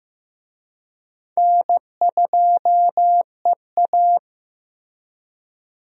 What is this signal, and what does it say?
Morse code, heard as a keyed tone, about 15 wpm, 700 Hz. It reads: N2EA
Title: USA Call Signs - 4 Characters - Repeated 3x - 15wpm